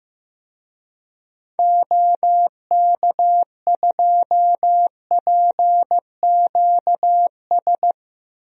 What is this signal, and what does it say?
Morse code, heard as a keyed tone, about 16 wpm, 700 Hz. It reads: OK2PQS
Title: DXCC-100 Call Signs - 6 Characters - 15wpm